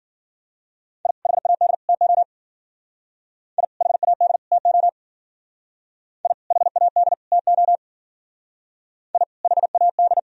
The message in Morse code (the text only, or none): I HAD TO I HAD TO I HAD TO I HAD